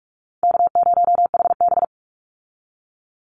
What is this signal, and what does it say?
Morse code, heard as a keyed tone, about 44 wpm, 700 Hz. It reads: K0HB
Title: Easy Call Signs Repeated 3x - 45wpm